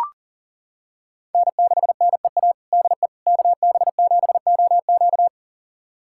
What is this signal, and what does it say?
Morse code, heard as a keyed tone, about 40 wpm, 700 Hz. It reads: N6NEA DE KB7OQ
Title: USA Callsign DE Callsign - 5 Characters - 40wpm